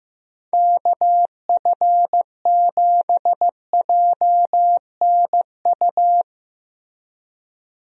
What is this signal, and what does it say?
Morse code, heard as a keyed tone, about 16 wpm, 700 Hz. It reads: KF7JNU
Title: USA Call Signs - 6 Characters - Repeated 2x - 15wpm